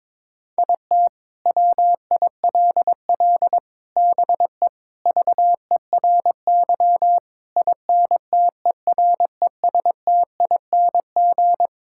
Morse code, heard as a keyed tone, about 21 wpm, 700 Hz. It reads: IT WILL BE VERY INTERESTING
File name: Sentences with 7-Word Phrases - 22wpm